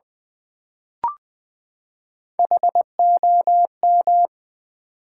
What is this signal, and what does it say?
Morse code, heard as a keyed tone, about 20 wpm, 700 Hz. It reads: HOM
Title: Build A Word - 3 Letters At A Time - 8-Letter Words - 20wpm